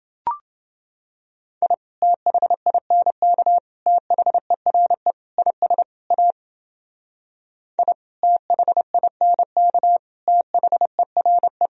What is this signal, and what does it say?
Morse code, heard as a keyed tone, about 29 wpm, 700 Hz. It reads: I THINK THERE IS A I THINK THERE